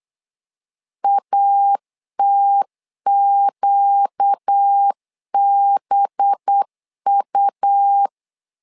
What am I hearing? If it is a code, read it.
Morse code, heard as a keyed tone, about 8 wpm, 800 Hz. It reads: ATQBU
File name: LEZIONI / 5 LBVC / 5 LB 8wpm 2816